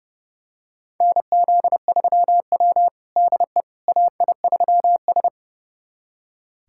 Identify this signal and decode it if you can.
Morse code, heard as a keyed tone, about 30 wpm, 700 Hz. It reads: NZ3W DE AI3S
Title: USA Callsign DE Callsign - 4 Characters - 30wpm